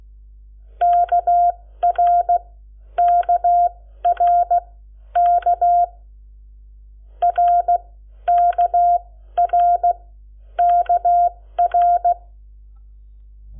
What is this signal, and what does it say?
Morse code, heard as a keyed tone, about 16 wpm, spaced as for 11 wpm, 700 Hz. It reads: KRKRK RKRKR
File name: Lecciones / CLASE Nº7